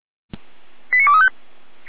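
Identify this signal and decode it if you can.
Five-tone selective call (ZVEI), heard as quick runs of five tones, tones of about 80 ms. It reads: 89235
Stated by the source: ZVEI five-tone selective call 89229 received on an SDR.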